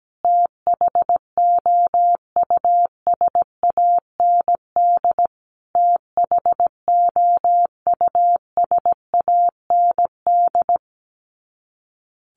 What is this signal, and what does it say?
Morse code, heard as a keyed tone, about 17 wpm, 700 Hz. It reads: THOUSAND THOUSAND
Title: Top 500 Words Repeated 17wpm